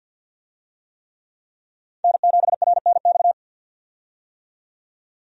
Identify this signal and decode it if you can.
Morse code, heard as a keyed tone, about 50 wpm, 700 Hz. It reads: N7RNX
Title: USA Call Signs - 5 Characters - 50wpm